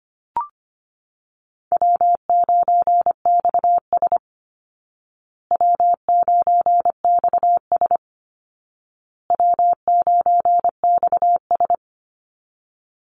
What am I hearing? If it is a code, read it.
Morse code, heard as a keyed tone, about 25 wpm, 700 Hz. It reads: W9XS W9XS W9XS
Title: Call Signs Repeated 2x - 25wpm